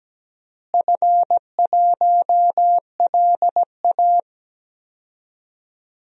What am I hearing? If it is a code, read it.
Morse code, heard as a keyed tone, about 17 wpm, 700 Hz. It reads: F1LA
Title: DXCC-100 Call Signs - 4 Characters - Repeated 3x - 17wpm